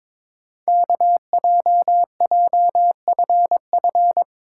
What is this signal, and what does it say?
Morse code, heard as a keyed tone, about 22 wpm, 700 Hz. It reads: KJJFF
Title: Code Groups - 5 Letters - 22wpm